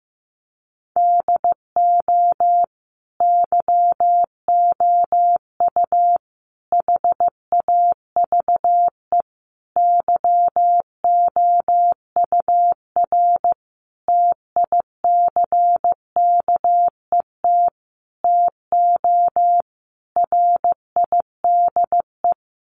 Morse code, heard as a keyed tone, about 15 wpm, 700 Hz. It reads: DO YOU HAVE YOUR TICKET TO RIDE
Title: Sentences from Top 1400 Words 15wpm